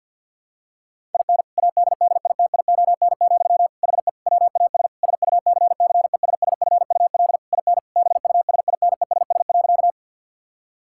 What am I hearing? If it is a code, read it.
Morse code, heard as a keyed tone, about 50 wpm, 700 Hz. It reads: IN ADDITION, HE WAS SUCCESSFUL IN BUSINESS.
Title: Sentences from Top 1000 Words - Review Part 6 - 50wpm